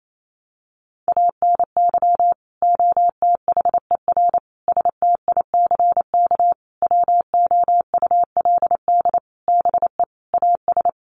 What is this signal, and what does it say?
Morse code, heard as a keyed tone, about 28 wpm, 700 Hz. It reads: ANY OTHER STICK WOULD BE AS
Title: Sentences from Top 1200 Words 28wpm